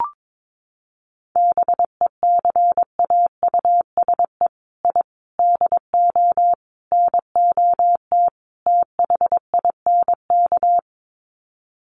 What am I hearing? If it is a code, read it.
Morse code, heard as a keyed tone, about 22 wpm, 700 Hz. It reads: BECAUSE I DO NOT THINK
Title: Sets of 5 Words Encore 22wpm